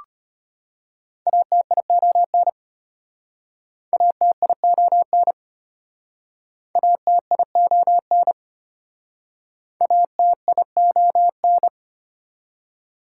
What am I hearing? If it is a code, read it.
Morse code, heard as a keyed tone, about 32 wpm, 700 Hz. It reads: ATION ATION ATION ATION